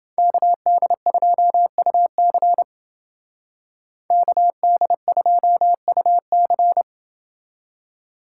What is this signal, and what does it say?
Morse code, heard as a keyed tone, about 30 wpm, 700 Hz. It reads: KD2UC KD2UC